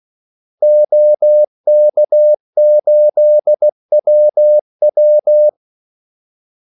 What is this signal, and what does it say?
Morse code, heard as a keyed tone, about 16 wpm, 600 Hz. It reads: OK8WW
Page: Kaldesignaler 16 wpm